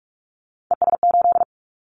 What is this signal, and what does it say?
Morse code, heard as a keyed tone, about 44 wpm, 700 Hz. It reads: ES8
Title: Callsign Prefixes for Top 100 DXCC - 45wpm